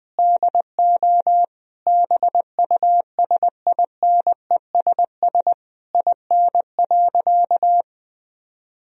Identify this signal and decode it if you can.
Morse code, heard as a keyed tone, about 20 wpm, 700 Hz. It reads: DO BUSINESS IN.
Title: Sentences from Top 1000 Words - Review Part 3 - 20wpm